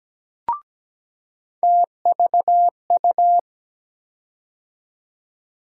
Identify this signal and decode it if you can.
Morse code, heard as a keyed tone, about 17 wpm, 700 Hz. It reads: TVU
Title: USA Call Sign Postfixes - 3 Letters 17wpm